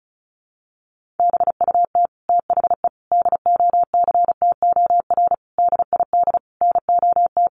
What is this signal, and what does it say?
Morse code, heard as a keyed tone, about 34 wpm, 700 Hz. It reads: BUT THE DOCTOR DID NOT
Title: Sentences from Top 1500 Words 35wpm